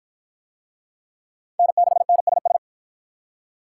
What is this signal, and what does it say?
Morse code, heard as a keyed tone, about 47 wpm, 700 Hz. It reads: N6NSS